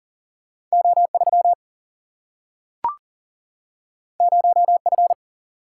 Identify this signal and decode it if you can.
Morse code, heard as a keyed tone, about 42 wpm, 700 Hz. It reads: O3 0F
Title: Single-Letter Single-Number 40wpm